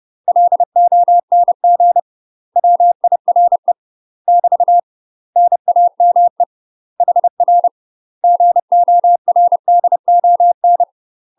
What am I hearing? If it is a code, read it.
Morse code, heard as a keyed tone, about 30 wpm, 700 Hz. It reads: LONG WIRE <BT> NAME HR GORDON
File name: UK MASTER QSOs TRAINING / 30WPM / UK MASTER QSOs 0047